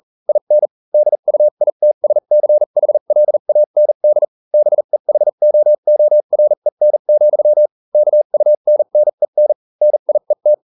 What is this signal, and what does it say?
Morse code, heard as a keyed tone, about 40 wpm, spaced as for 35 wpm, 600 Hz. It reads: IN DUITSCHLAND BEHOOREN, KUNNEN NIET